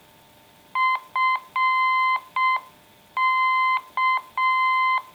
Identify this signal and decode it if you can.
Morse code, heard as a keyed tone, about 6 wpm, 1040 Hz. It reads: FK